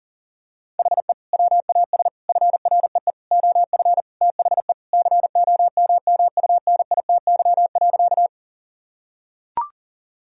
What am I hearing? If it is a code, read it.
Morse code, heard as a keyed tone, about 40 wpm, 700 Hz. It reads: HE WAS FREE OF THE COMMUNITY.
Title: Sentences from Top 1000 Words - Review Part 4 - 40wpm